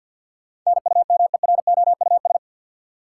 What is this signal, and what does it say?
Morse code, heard as a keyed tone, about 52 wpm, 700 Hz. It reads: NUMEROUS
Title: Sentences from Top 1400 Words - New Word Drill - 50wpm